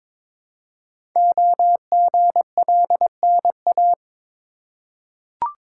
Morse code, heard as a keyed tone, about 22 wpm, 700 Hz. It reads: OGLNA
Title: Code Groups - 5 Letters - 22wpm